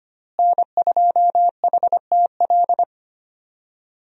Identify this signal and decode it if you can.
Morse code, heard as a keyed tone, about 25 wpm, 700 Hz. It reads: N2HTL
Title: USA Call Signs - 5 Characters - Repeated 3x - 25wpm